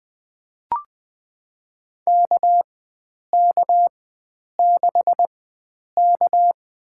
Morse code, heard as a keyed tone, about 20 wpm, spaced as for 9 wpm, 700 Hz. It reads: KK6K
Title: USA Call Signs - 4x Character Spacing - 20wpm